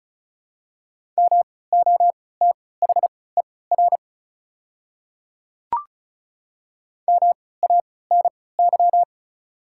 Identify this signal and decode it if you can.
Morse code, heard as a keyed tone, about 36 wpm, spaced as for 20 wpm, 700 Hz. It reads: MOTHER MANY
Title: 3x Character Spacing - Top 200 Words - 35wpm